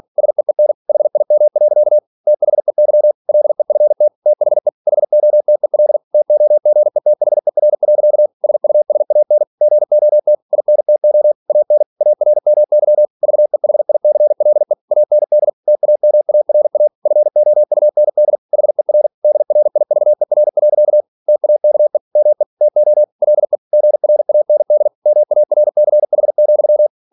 Morse code, heard as a keyed tone, about 46 wpm, 600 Hz. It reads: SEEN HIM. THEY LEFT THE HOTEL TOGETHER. SUSAN GOT INTO AN ARMY VEHICLE AND TAMARA FOUND HER DRIVER. TAKE ME TO LE GRAND MARCH,